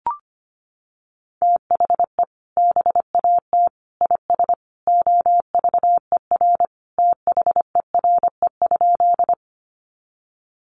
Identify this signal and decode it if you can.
Morse code, heard as a keyed tone, about 25 wpm, 700 Hz. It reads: THE BAT IS OVER THERE?
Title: Sentences from Top 1500 Words 25wpm